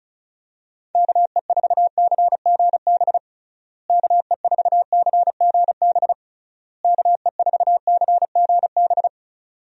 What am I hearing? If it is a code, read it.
Morse code, heard as a keyed tone, about 34 wpm, 700 Hz. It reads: KE4CGB KE4CGB KE4CGB
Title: USA Call Signs - 6 Characters - Repeated 3x - 35wpm